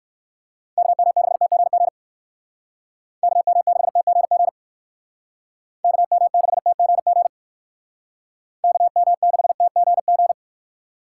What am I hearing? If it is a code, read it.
Morse code, heard as a keyed tone, about 58 wpm, 700 Hz. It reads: KM6TGG KM6TGG KM6TGG KM6TGG